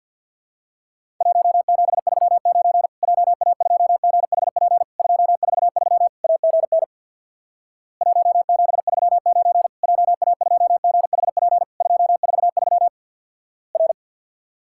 Morse code, heard as a keyed tone, about 50 wpm, 700 Hz. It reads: 1739 JA2GHP 243 AGN 1739 JA2GHP 243 R
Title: Worked All Europe DX Contest - QTC Exchange with Repeats - 50wpm